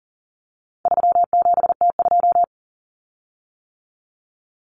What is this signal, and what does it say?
Morse code, heard as a keyed tone, about 40 wpm, 700 Hz. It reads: 37T2
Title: Common Cut Numbers - 4-Digit 40wpm